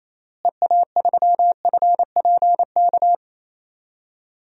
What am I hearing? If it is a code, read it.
Morse code, heard as a keyed tone, about 28 wpm, 700 Hz. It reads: EA3FPK
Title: DXCC-100 Call Signs - 6 Characters - Repeated 2x - 28wpm